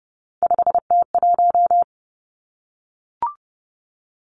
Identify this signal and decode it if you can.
Morse code, heard as a keyed tone, about 30 wpm, 700 Hz. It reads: 5T1
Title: Three Challenging Letter-Number Combinations 30wpm